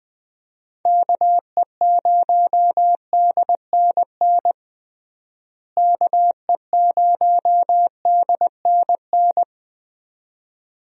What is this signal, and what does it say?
Morse code, heard as a keyed tone, about 20 wpm, 700 Hz. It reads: KE0DNN KE0DNN
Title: USA Call Signs - 6 Characters - Repeated 2x - 20wpm